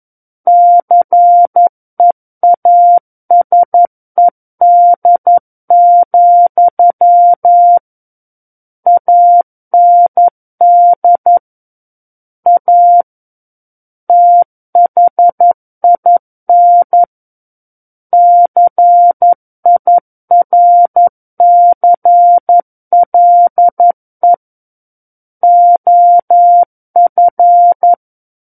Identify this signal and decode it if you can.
Morse code, heard as a keyed tone, about 11 wpm, 700 Hz. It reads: CEASED, AND A THIN CIRCLE OF
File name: Book One / 02-Chapter 2 - 11 WPM